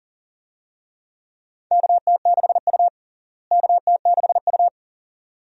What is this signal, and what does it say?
Morse code, heard as a keyed tone, about 40 wpm, 700 Hz. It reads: KT6U KT6U
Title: USA Call Signs - 4 Characters - Repeated 2x - 40wpm